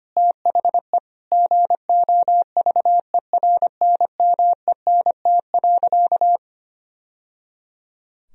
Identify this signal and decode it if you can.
Morse code, heard as a keyed tone, about 25 wpm, 700 Hz. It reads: THE GOVERNMENT.
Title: Sentences from Top 1000 Words - Review Part 3 - 25wpm